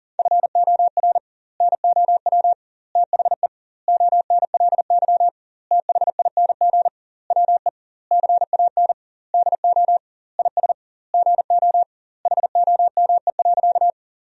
Morse code, heard as a keyed tone, about 40 wpm, 700 Hz. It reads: FOR NOW THE ONLY THING WE CAN DO IS GO HOME.
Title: 2x Word-Spacing - Sentences from Top 200 Words - 40wpm